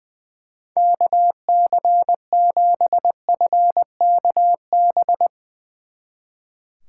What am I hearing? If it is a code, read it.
Morse code, heard as a keyed tone, about 21 wpm, 700 Hz. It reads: KC7FKB
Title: Speed Racing - USA Call Signs 20wpm